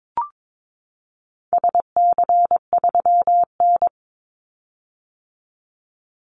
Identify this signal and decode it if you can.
Morse code, heard as a keyed tone, about 22 wpm, 700 Hz. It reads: SC3N
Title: Call Signs - 22wpm